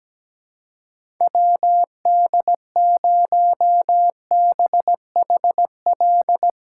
Morse code, heard as a keyed tone, about 17 wpm, 700 Hz. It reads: WD0BHL
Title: USA Call Signs - 6 Characters - 2x Character Spacing - 17wpm